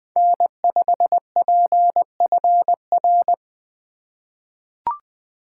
Morse code, heard as a keyed tone, about 20 wpm, 700 Hz. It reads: N5PFR
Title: USA Call Signs - 5 Characters - 2x Character Spacing - 20wpm